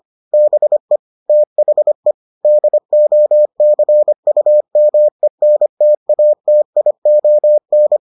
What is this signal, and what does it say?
Morse code, heard as a keyed tone, about 25 wpm, 600 Hz. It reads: BE THE DOCUMENTATION